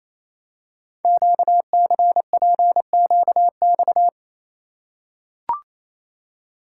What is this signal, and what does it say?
Morse code, heard as a keyed tone, about 28 wpm, 700 Hz. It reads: QCPQX
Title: Code Groups - 5 Letters - 28wpm